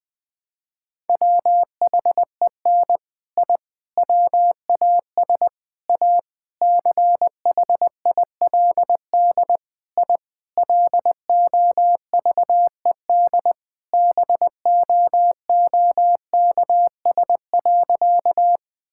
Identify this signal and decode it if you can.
Morse code, heard as a keyed tone, about 20 wpm, 700 Hz. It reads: WHEN I WAS A CHILD I LOVED BOOKS.
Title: Sentences with 3-Word Phrases - 20wpm